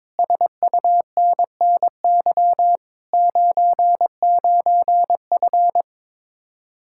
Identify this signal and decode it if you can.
Morse code, heard as a keyed tone, about 22 wpm, 700 Hz. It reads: SUNNY 99F
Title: Speed-Racing Common QSO Phrases - 22wpm